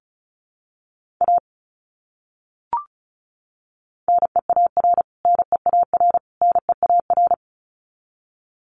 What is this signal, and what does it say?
Morse code, heard as a keyed tone, about 34 wpm, 700 Hz. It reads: A NEAR NEAR NEAR
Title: Top 200 Words Repeated 35wpm